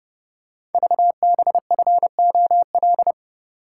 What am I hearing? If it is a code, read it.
Morse code, heard as a keyed tone, about 28 wpm, 700 Hz. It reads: VBFOL
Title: Code Groups - 5 Letters - 30wpm